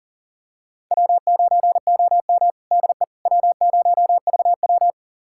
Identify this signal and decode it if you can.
Morse code, heard as a keyed tone, about 40 wpm, 700 Hz. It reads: W9OM DE W0VW
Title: USA Callsign DE Callsign - 4 Characters - 40wpm